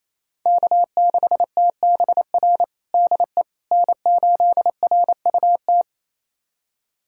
Morse code, heard as a keyed tone, about 28 wpm, 700 Hz. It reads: K6TBR DE N8RUT
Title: USA Callsign DE Callsign - 5 Characters - 28wpm